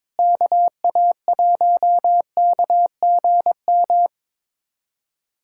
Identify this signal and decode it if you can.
Morse code, heard as a keyed tone, about 22 wpm, 700 Hz. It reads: KA1KGM
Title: USA Callsign DE Callsign - 6 Characters - 22wpm